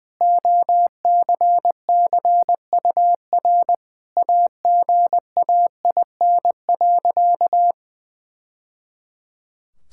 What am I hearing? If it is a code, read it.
Morse code, heard as a keyed tone, about 21 wpm, 700 Hz. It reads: OCCUR AGAIN.
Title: Sentences from Top 1000 Words - Review Part 6 - 20wpm